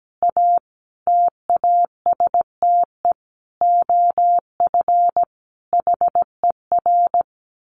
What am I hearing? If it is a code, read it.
Morse code, heard as a keyed tone, about 17 wpm, 700 Hz. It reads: A TASTE OF HER
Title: Sentences from Top 1100 Words 17wpm